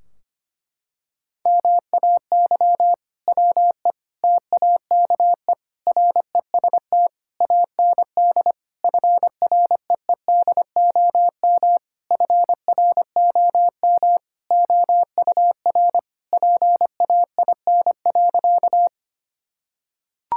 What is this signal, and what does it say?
Morse code, heard as a keyed tone, about 25 wpm, 700 Hz. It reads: MAY WE TAKE REST AND FREEDOM FROM OUR PAIN.
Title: Sentences from Top 1600 Words 25wpm